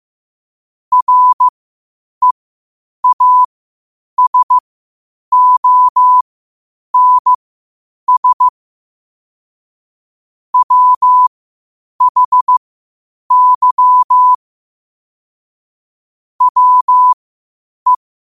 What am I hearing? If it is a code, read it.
Morse code, heard as a keyed tone, about 15 wpm, spaced as for 8 wpm, 1000 Hz. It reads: REASONS WHY WE
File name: qotdfiles / 8wpm